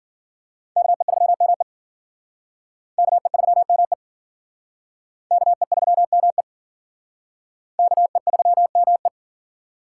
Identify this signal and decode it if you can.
Morse code, heard as a keyed tone, about 50 wpm, 700 Hz. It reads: KE3ME KE3ME KE3ME KE3ME